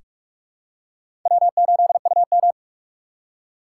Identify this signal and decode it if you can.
Morse code, heard as a keyed tone, about 41 wpm, 700 Hz. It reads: W8UM
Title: USA Call Signs - 2x Character Spacing - 45wpm